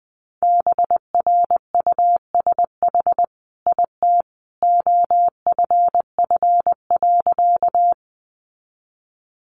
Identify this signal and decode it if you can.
Morse code, heard as a keyed tone, about 20 wpm, 700 Hz. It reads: BRUSH IT OFF.
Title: Sentences from Top 1100 Words 20wpm